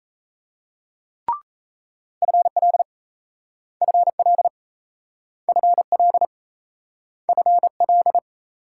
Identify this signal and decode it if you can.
Morse code, heard as a keyed tone, about 37 wpm, 700 Hz. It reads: FL FL FL FL